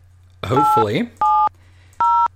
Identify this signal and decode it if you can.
DTMF, heard as telephone touch tones, tones of about 290 ms, with speech. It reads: 870